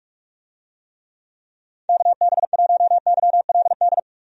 Morse code, heard as a keyed tone, about 46 wpm, 700 Hz. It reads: KB1YLD
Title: USA Call Signs - 6 Characters - Repeated 2x - 45wpm